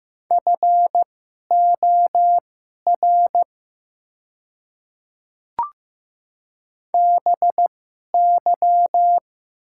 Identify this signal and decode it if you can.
Morse code, heard as a keyed tone, about 15 wpm, spaced as for 11 wpm, 700 Hz. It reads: FOR BY
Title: 2x Character Spacing - Top 100 Words - 15wpm